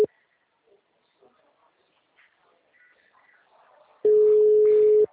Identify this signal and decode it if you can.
Ringback tone (Europe, 425 Hz).
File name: out- -2017- - - .29778